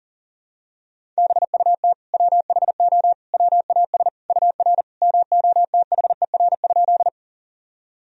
Morse code, heard as a keyed tone, about 40 wpm, 700 Hz. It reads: BUT WHO WAS UR MOTHER?
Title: Sentences from Top 200 Words and Abbreviations 40wpm